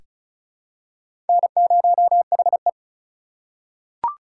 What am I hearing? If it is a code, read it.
Morse code, heard as a keyed tone, about 36 wpm, 700 Hz. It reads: N0HE
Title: USA Call Signs - 2x Character Spacing - 35wpm